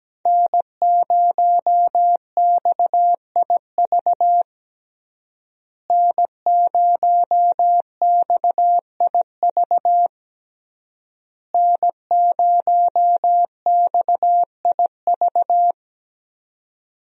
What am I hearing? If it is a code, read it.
Morse code, heard as a keyed tone, about 17 wpm, 700 Hz. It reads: N0XIV N0XIV N0XIV
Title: USA Call Signs - 5 Characters - Repeated 3x - 17wpm